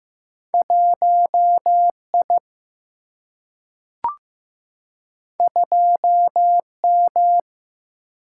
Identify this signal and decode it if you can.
Morse code, heard as a keyed tone, about 15 wpm, 700 Hz. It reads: 1 I 2 M
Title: Single-Letter Single-Number 15wpm